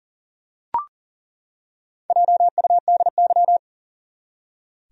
Morse code, heard as a keyed tone, about 40 wpm, 700 Hz. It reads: JUDY
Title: USA Common Names - Max 4 Letters - 40wpm